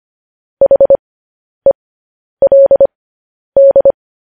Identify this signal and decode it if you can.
Morse code, heard as a keyed tone, about 25 wpm, spaced as for 10 wpm, 550 Hz. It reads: HELD
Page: news headlines in morse code at 10 wpm